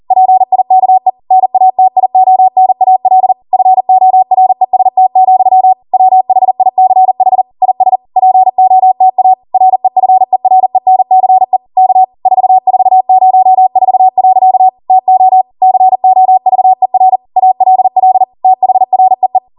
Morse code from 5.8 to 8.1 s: WHICH IS